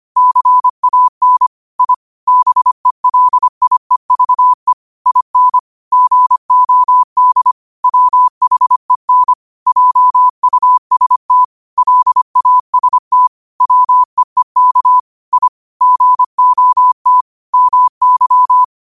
Morse code, heard as a keyed tone, about 25 wpm, 1000 Hz. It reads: CAN I BELIEVE IN GOD WHEN JUST LAST WEEK I GOT MY
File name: qotdfiles / 25wpm